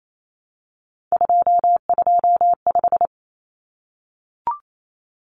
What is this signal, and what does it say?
Morse code, heard as a keyed tone, about 28 wpm, 700 Hz. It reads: 225
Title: Common Cut Numbers - 3-Digit 28wpm